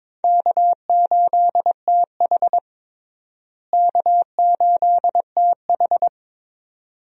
Morse code, heard as a keyed tone, about 21 wpm, 700 Hz. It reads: K8TH K8TH
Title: USA Call Signs - 4 Characters - Repeated 3x - 22wpm